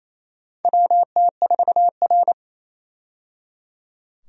Morse code, heard as a keyed tone, about 28 wpm, 700 Hz. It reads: WT4R
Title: USA Call Signs - 4 Characters - 28wpm